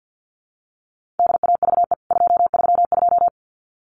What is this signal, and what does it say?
Morse code, heard as a keyed tone, about 52 wpm, 700 Hz. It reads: DAVE 232
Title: CWT Exchange 50wpm